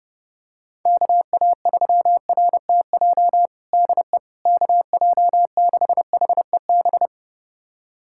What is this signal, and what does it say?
Morse code, heard as a keyed tone, about 30 wpm, 700 Hz. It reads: KA3RTJ DE KJ6HEB
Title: USA Callsign DE Callsign 30wpm